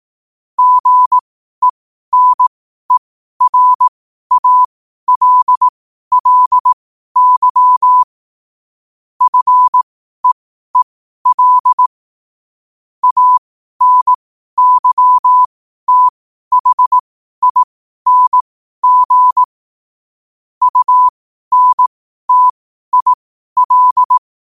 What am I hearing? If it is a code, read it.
Morse code, heard as a keyed tone, about 18 wpm, spaced as for 12 wpm, 1000 Hz. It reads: GENERALLY FEEL ANYTHING UNTIL
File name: qotdfiles / 12wpm